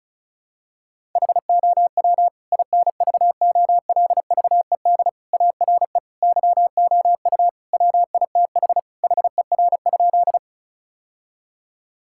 Morse code, heard as a keyed tone, about 34 wpm, 690 Hz. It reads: HOW INVOLVED ARE YOU WITH HER?
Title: Sentences from Top 1000 Words - Review Part 5 - 35wpm